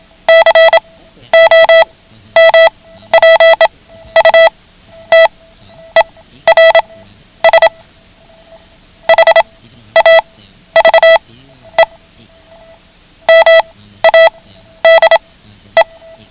Morse code, heard as a keyed tone, about 27 wpm, spaced as for 12 wpm, 660 Hz. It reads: COMPUTERS HAVE MADE
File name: ham radio / CodeQuick / Code Quick 09